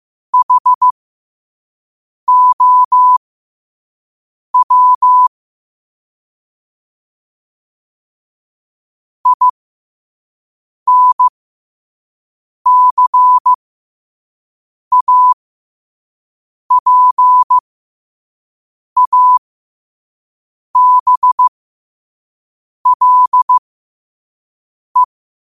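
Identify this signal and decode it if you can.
Morse code, heard as a keyed tone, about 15 wpm, spaced as for 5 wpm, 1000 Hz. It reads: HOW INCAPABLE
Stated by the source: Morse code practice at 5 wpm.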